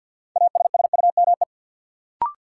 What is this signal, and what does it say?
Morse code, heard as a keyed tone, about 53 wpm, 700 Hz. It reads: ASSUME
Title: Sentences from Top 1500 Words - New Word Drill - 50wpm